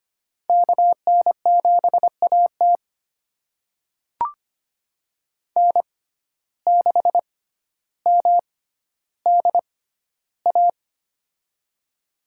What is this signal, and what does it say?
Morse code, heard as a keyed tone, about 25 wpm, spaced as for 13 wpm, 700 Hz. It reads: KN7AT N6MDA
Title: USA Call Signs - 5 Characters - 6x Character Spacing - 25wpm